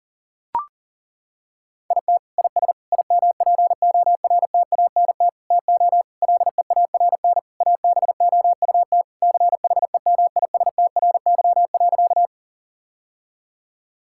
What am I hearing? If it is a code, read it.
Morse code, heard as a keyed tone, about 37 wpm, 700 Hz. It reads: IT IS IMPORTANT TO LEARN ABOUT CHEMISTRY.
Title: Sentences from Top 1000 Words - Review Part 6 - 40wpm